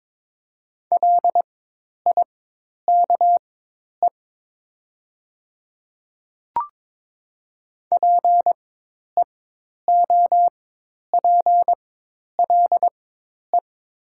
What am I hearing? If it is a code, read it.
Morse code, heard as a keyed tone, about 22 wpm, spaced as for 10 wpm, 700 Hz. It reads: LIKE PEOPLE
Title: 4x Character Spacing - Top 100 Words - 22wpm